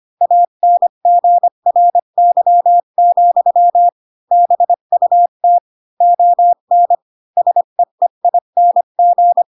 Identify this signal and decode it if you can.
Morse code, heard as a keyed tone, about 25 wpm, 700 Hz. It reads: ANGRY, BUT ON SEEING
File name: Pinocchio / Pinocchio ch7 20 25 ik1hge